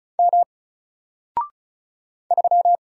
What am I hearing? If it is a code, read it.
Morse code, heard as a keyed tone, about 36 wpm, 700 Hz. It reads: M 3
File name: Single-Letter-Number-35wpm